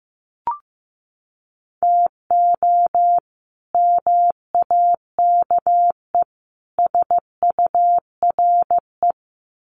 Morse code, heard as a keyed tone, about 15 wpm, 700 Hz. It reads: TO MAKE SURE
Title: Sets of 3 Words 15wpm